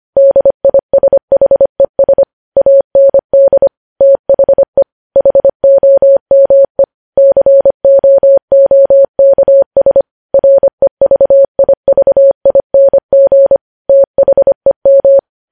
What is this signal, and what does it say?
Morse code, heard as a keyed tone, about 25 wpm, 550 Hz. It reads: DISHES AND THE HOME COOKS REVIVING THEM
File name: podcast / News headlines-1x-25x25